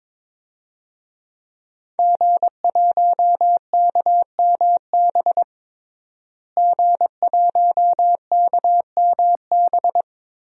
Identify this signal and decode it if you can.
Morse code, heard as a keyed tone, about 22 wpm, 700 Hz. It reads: G1KMB G1KMB
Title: DXCC-100 Call Signs - 5 Characters - Repeated 2x - 22wpm